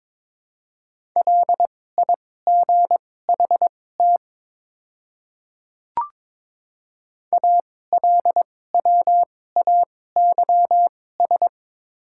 Morse code, heard as a keyed tone, about 22 wpm, spaced as for 16 wpm, 690 Hz. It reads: LIGHT ALWAYS
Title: 2x Character Spacing - Top 300 Words - 22wpm